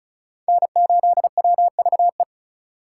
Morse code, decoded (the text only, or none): N8WVE